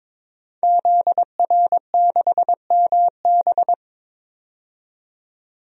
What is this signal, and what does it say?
Morse code, heard as a keyed tone, about 22 wpm, 700 Hz. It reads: ZR6MB
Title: DXCC-100 Call Signs - 5 Characters - 22wpm